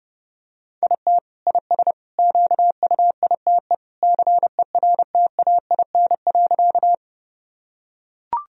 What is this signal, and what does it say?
Morse code, heard as a keyed tone, about 30 wpm, 700 Hz. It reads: IT IS QUITE CERTAIN.
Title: Sentences from Top 1000 Words - Review Part 2 - 30wpm